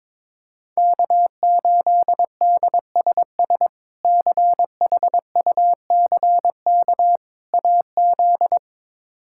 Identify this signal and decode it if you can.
Morse code, heard as a keyed tone, about 22 wpm, 700 Hz. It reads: K8DSS CHUCK AZ
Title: K1USN SST QSO Practice - 22wpm